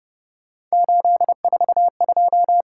Morse code, heard as a keyed tone, about 31 wpm, 700 Hz. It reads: 842
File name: Speed-Racing - Three Letter-Number Combinations - 25wpm